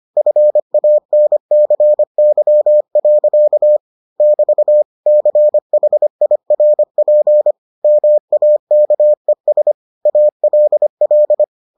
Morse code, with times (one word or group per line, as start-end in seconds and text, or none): 0.2-3.8 FANCY.
4.2-4.8 <BT>
5.1-7.5 CHIRP
7.8-9.7 MAKES
10.1-11.4 ALL